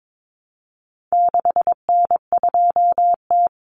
Morse code, heard as a keyed tone, about 22 wpm, 700 Hz. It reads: 6N2T
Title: Common Cut Numbers - 4-Digit 22wpm